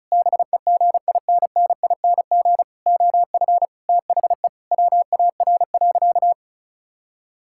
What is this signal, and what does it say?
Morse code, heard as a keyed tone, about 36 wpm, 700 Hz. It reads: BEGINNING OF THE WAR.
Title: Sentences from Top 1000 Words - Review Part 6 - 35wpm